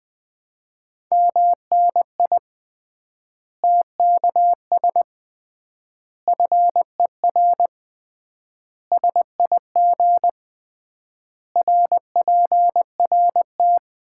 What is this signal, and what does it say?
Morse code, heard as a keyed tone, about 20 wpm, 700 Hz. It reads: MNI TKS FER SIG RPRT
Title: Common QSO Phrases - 3x Word-Spacing - 20wpm